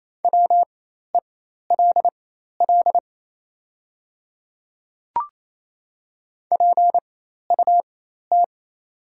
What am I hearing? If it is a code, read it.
Morse code, heard as a keyed tone, about 27 wpm, spaced as for 13 wpm, 690 Hz. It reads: WELL PUT
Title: 4x Character Spacing - Top 200 Words - 28wpm